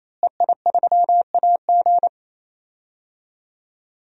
Morse code, heard as a keyed tone, about 27 wpm, 700 Hz. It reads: EI3AG
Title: DXCC-100 Call Signs - 5 Characters - Repeated 3x - 28wpm